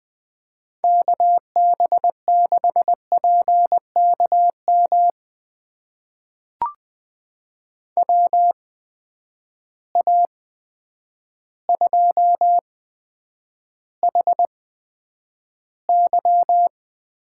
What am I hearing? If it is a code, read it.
Morse code, heard as a keyed tone, about 20 wpm, spaced as for 9 wpm, 700 Hz. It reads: KB6PKM WA2HY
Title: USA Call Signs - 6 Characters - 8x Character Spacing - 20wpm